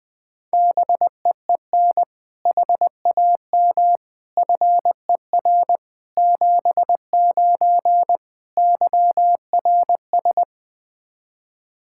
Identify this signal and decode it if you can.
Morse code, heard as a keyed tone, about 20 wpm, 700 Hz. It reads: BEEN HAM FER 79 YRS
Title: Practice QSO - Exchange 3 - More Info - 20wpm